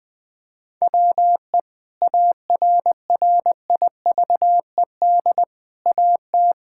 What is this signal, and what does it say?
Morse code, heard as a keyed tone, about 20 wpm, 700 Hz. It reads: WE ARRIVED AT
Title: Sentences with 4-Word Phrases - 20wpm